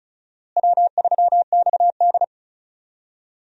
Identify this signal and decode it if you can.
Morse code, heard as a keyed tone, about 36 wpm, 700 Hz. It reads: W3XD
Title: USA Call Signs - 4 Characters - Repeated 3x - 35wpm